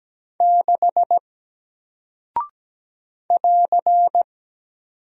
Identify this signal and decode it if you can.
Morse code, heard as a keyed tone, about 17 wpm, 700 Hz. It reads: 6 <AR>
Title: Single Character (Letter:Number:Punctuation:Prosign) Alt 17wpm